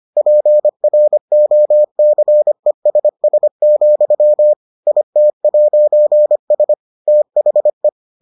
Morse code, heard as a keyed tone, about 25 wpm, 600 Hz. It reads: PROCESS, IT'S THE